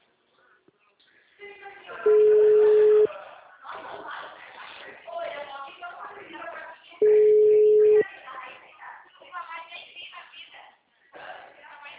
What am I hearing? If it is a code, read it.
Ringback tone (Europe, 425 Hz), with speech.
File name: out- -2011- - - .50681